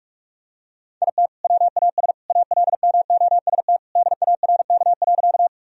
Morse code, heard as a keyed tone, about 42 wpm, 700 Hz. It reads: IT WAS ALMOST DARK.
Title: Sentences from Top 1000 Words - Review Part 4 - 45wpm